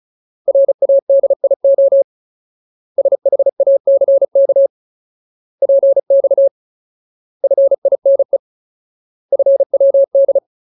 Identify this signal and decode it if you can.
Morse code, heard as a keyed tone, about 34 wpm, 550 Hz. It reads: RADIO SHACK PX FINE FWD
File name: Ultimate Word List / 35WPM Ultimate